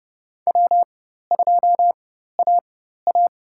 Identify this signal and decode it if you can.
Morse code, heard as a keyed tone, about 30 wpm, spaced as for 14 wpm, 700 Hz. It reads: W2AA
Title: USA Call Signs - 4x Character Spacing - 30wpm